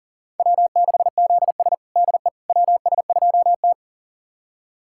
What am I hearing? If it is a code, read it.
Morse code, heard as a keyed tone, about 37 wpm, 700 Hz. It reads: W6ZS DE WS2T
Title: USA Callsign DE Callsign - 4 Characters - 40wpm